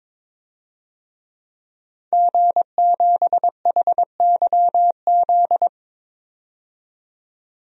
Morse code, heard as a keyed tone, about 22 wpm, 700 Hz. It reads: G7HYZ
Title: DXCC-100 Call Signs - 5 Characters - 22wpm